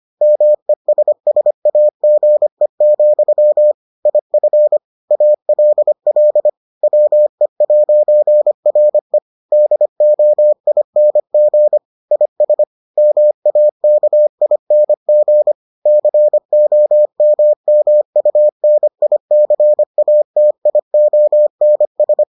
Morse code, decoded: MESSAGE, IF ALL WE'RE DOING IS MAKING COMMUNICATIONS

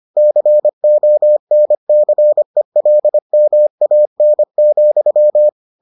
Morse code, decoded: CONCELMAN,

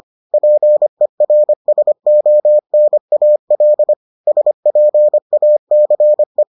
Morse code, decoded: PERSONAL SPACE

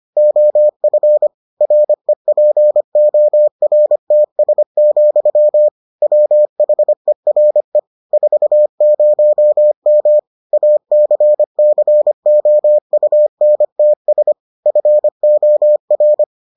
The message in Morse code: OF REPORTS, WHERE 40M ACCOUNTS FOR

